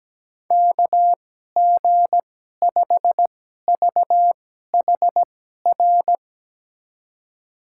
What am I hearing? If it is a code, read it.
Morse code, heard as a keyed tone, about 17 wpm, spaced as for 12 wpm, 700 Hz. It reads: KG5VHR
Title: USA Call Signs - 6 Characters - 2x Character Spacing - 17wpm